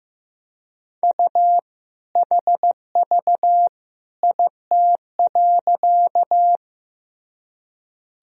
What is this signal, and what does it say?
Morse code, heard as a keyed tone, about 15 wpm, 700 Hz. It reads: U HV IT.
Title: Sentences from Top 100 Words and Abbreviations 15wpm